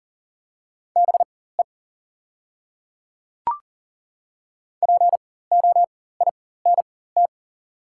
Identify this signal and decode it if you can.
Morse code, heard as a keyed tone, about 37 wpm, spaced as for 18 wpm, 690 Hz. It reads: BE POINT
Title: 4x Character Spacing - Top 200 Words - 40wpm